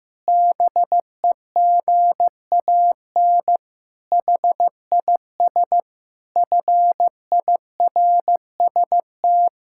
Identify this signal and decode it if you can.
Morse code, heard as a keyed tone, about 15 wpm, 700 Hz. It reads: BEGAN HIS FIRST
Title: Pangram Challenge v2 - 15wpm